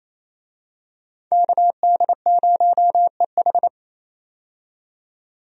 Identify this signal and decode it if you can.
Morse code, heard as a keyed tone, about 28 wpm, 700 Hz. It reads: KD0EH
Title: USA Call Signs - 5 Characters - 28wpm